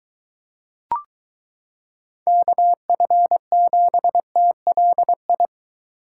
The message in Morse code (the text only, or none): KF7TLI